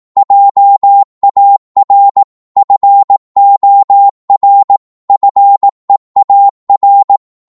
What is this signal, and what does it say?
Morse code, heard as a keyed tone, about 18 wpm, 800 Hz. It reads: JAR FOR FEAR